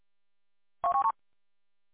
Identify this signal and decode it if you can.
There is no code.